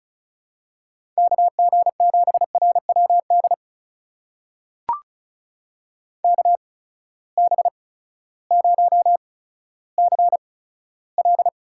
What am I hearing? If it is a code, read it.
Morse code, heard as a keyed tone, about 36 wpm, spaced as for 16 wpm, 700 Hz. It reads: KG7RWD KB0CL
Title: USA Call Signs - 6 Characters - 8x Character Spacing - 35wpm